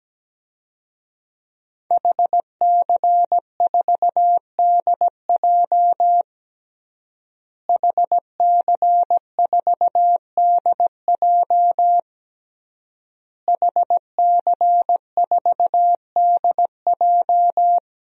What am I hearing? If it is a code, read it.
Morse code, heard as a keyed tone, about 17 wpm, 700 Hz. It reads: HC4DJ HC4DJ HC4DJ
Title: DXCC-100 Call Signs - 5 Characters - Repeated 3x - 17wpm